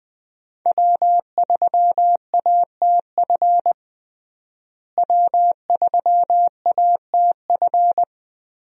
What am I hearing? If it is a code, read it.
Morse code, heard as a keyed tone, about 20 wpm, 700 Hz. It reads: W3ATF W3ATF
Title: USA Call Signs - 5 Characters - Repeated 2x - 20wpm